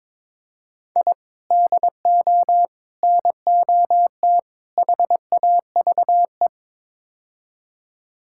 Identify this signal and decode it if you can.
Morse code, heard as a keyed tone, about 22 wpm, 700 Hz. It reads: I DO NOT HAVE
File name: practice / 22wpm / Build A Phrase - 5-Word Phrases - 22wpm